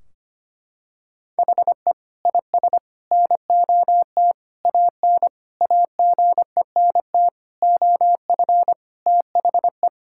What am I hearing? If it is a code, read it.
Morse code, heard as a keyed tone, about 25 wpm, 700 Hz. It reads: HE IS NOT AN AGENT OF THE
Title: Sentences from Top 1600 Words 25wpm